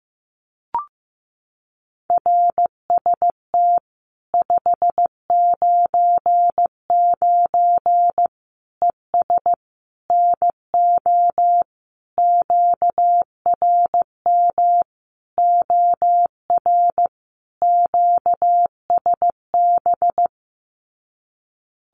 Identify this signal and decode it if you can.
Morse code, heard as a keyed tone, about 15 wpm, 700 Hz. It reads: RST 599 ES NO QRM OR QSB
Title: Common QSO Phrases 15wpm